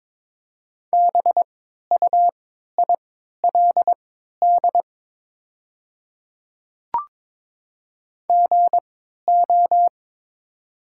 Morse code, heard as a keyed tone, about 21 wpm, spaced as for 12 wpm, 700 Hz. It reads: BUILD GO
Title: 3x Character Spacing - Top 200 Words - 22wpm